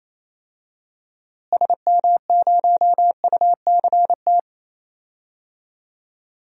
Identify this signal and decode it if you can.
Morse code, heard as a keyed tone, about 27 wpm, 700 Hz. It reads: SM0UCT
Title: DXCC-100 Call Signs - 6 Characters - Repeated 3x - 28wpm